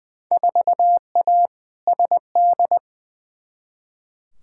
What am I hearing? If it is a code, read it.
Morse code, heard as a keyed tone, about 20 wpm, 700 Hz. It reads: 4A SD
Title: Field Day Exchanges - 20wpm